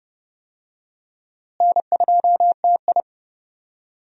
Morse code, heard as a keyed tone, about 30 wpm, 700 Hz. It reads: N2TI
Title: USA Call Signs - 4 Characters - Repeated 2x - 30wpm